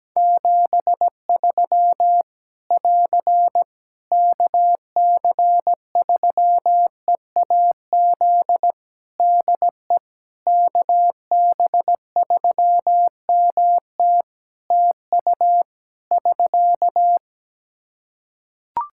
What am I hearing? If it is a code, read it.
Morse code, heard as a keyed tone, about 17 wpm, 700 Hz. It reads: 73 <AR> KC3EAZ DE KB3MT TU <SK>
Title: Practice QSO - Exchange 4 - Ending - 17wpm